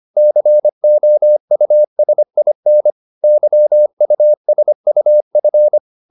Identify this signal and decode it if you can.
Morse code, heard as a keyed tone, about 25 wpm, 600 Hz. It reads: COUSIN YUSUF